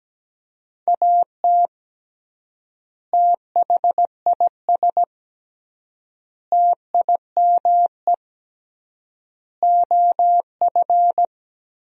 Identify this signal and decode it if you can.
Morse code, heard as a keyed tone, about 17 wpm, 700 Hz. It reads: AT THIS TIME OF
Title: 3x Word-Spacing - Sentences from Top 300 Words - 17wpm